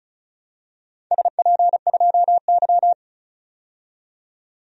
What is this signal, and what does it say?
Morse code, heard as a keyed tone, about 34 wpm, 700 Hz. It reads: SP2Y
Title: DXCC-100 Call Signs - 4 Characters - 35wpm